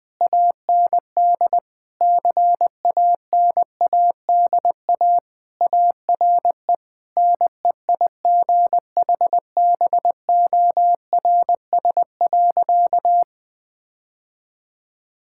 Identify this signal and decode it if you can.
Morse code, heard as a keyed tone, about 20 wpm, 700 Hz. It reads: AND CANADA ARE NEIGHBORS.
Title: Sentences with 4-Word Phrases - Spoken Beginning - 20wpm